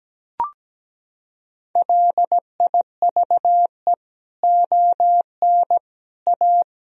Morse code, heard as a keyed tone, about 17 wpm, 700 Hz. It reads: LIVE ON A
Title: Sentences from Top 1000 Words - Review Part 1 - 17wpm